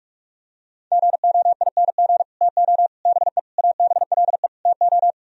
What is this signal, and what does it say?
Morse code, heard as a keyed tone, about 44 wpm, 700 Hz. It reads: GOING TO BE ABLE TO